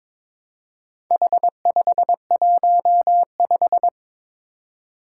Morse code, heard as a keyed tone, about 22 wpm, 700 Hz. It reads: H515
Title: Four Challenging Letter-Number Combinations 22wpm